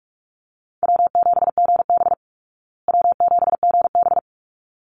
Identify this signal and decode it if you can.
Morse code, heard as a keyed tone, about 44 wpm, 700 Hz. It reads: W7GB W7GB
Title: Easy Call Signs Repeated 2x - 45wpm